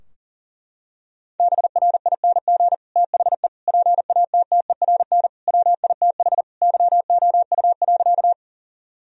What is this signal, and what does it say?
Morse code, heard as a keyed tone, about 40 wpm, 700 Hz. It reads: BRING THE PATTERN WITH YOU.
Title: Sentences from Top 1000 Words - Review Part 2 - 40wpm